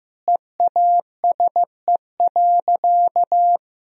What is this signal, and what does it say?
Morse code, heard as a keyed tone, about 16 wpm, 700 Hz. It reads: EASE.
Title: Sentences from Top 1000 Words - Review Part 1 - 15wpm